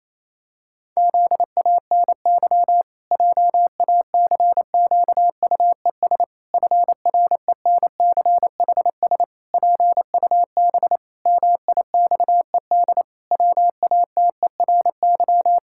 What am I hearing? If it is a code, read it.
Morse code, heard as a keyed tone, about 28 wpm, 700 Hz. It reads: ZANY JACQUES FRENCHS PUB MIXED WATERY
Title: Pangram Challenge v2 - 28wpm